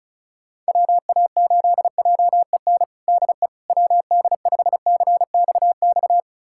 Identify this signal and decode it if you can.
Morse code, heard as a keyed tone, about 36 wpm, 700 Hz. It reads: WA8JEN DE WD5CXX
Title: USA Callsign DE Callsign 35wpm